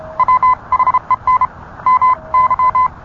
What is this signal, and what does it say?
Morse code, heard as a keyed tone, about 30 wpm, 1030 Hz. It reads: WHEN MY